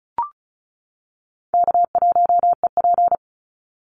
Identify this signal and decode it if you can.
Morse code, heard as a keyed tone, about 34 wpm, 700 Hz. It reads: K1EP
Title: Call Signs - 35wpm